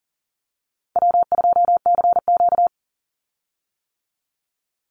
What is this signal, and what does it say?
Morse code, heard as a keyed tone, about 37 wpm, 700 Hz. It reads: W2CQ
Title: Call Signs - 40wpm